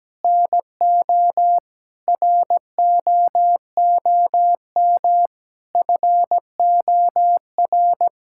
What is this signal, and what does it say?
Morse code, heard as a keyed tone, about 17 wpm, 700 Hz. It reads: NO ROOM FOR
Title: Sentences from Top 1000 Words - Review Part 5 - 17wpm